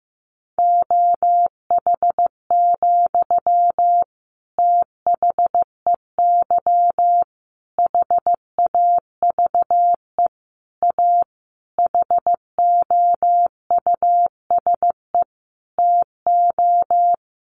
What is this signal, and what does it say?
Morse code, heard as a keyed tone, about 15 wpm, 700 Hz. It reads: OH, THEY HAVE A HOUSE TO
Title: Sentences from Top 800 Words 15wpm